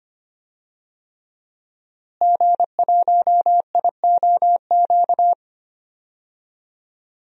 Morse code, heard as a keyed tone, about 25 wpm, 700 Hz. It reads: G1IOQ
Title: DXCC-100 Call Signs - 5 Characters - Repeated 3x - 25wpm